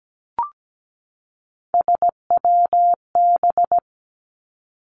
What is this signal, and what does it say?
Morse code, heard as a keyed tone, about 17 wpm, 700 Hz. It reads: SWB
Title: Three Challenging Letter-Number Combinations 17wpm